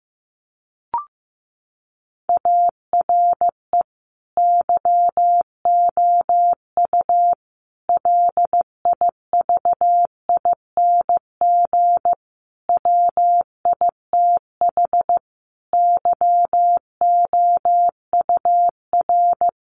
Morse code, heard as a keyed tone, about 15 wpm, 700 Hz. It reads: ARE YOU LIVING WITH YOUR
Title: Sentences from Top 1100 Words 15wpm